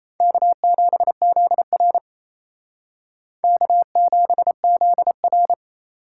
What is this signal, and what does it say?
Morse code, heard as a keyed tone, about 35 wpm, 700 Hz. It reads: K7ZR K7ZR
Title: Speed Racing - USA 1x2 Call Signs 28wpm